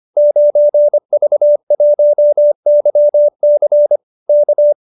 Morse code, heard as a keyed tone, about 25 wpm, 570 Hz. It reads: 9V1YC K